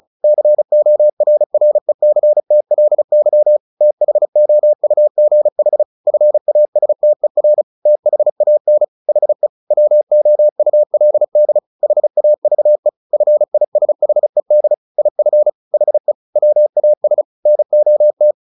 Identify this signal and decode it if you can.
Morse code, heard as a keyed tone, about 34 wpm, 600 Hz. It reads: CORRECTLY THOUGH FASTER THAN HE WOULD HAVE FISHED IF HE WAS NOT
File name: the old man and the sea / the old man and the sea 1 35wpm